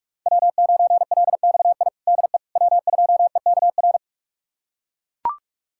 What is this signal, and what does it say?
Morse code, heard as a keyed tone, about 42 wpm, 700 Hz. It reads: W9LXI DE W2EKR
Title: USA Callsign DE Callsign - 5 Characters - 45wpm